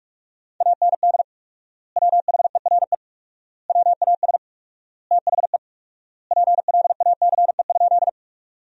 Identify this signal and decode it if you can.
Morse code, heard as a keyed tone, about 46 wpm, 700 Hz. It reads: AND WHERE WAS THE PLACE?
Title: 4x Word-Spacing - Sentences from Top 200 Words - 45wpm